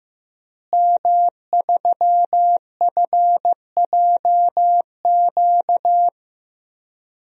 Morse code, heard as a keyed tone, about 15 wpm, 700 Hz. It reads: M3FJQ
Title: DXCC-100 Call Signs - 5 Characters - Repeated 2x - 15wpm